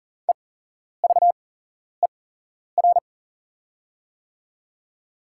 Morse code, heard as a keyed tone, about 40 wpm, spaced as for 11 wpm, 700 Hz. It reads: EVER
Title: 8x Character Spacing - Top 500 Words - 40wpm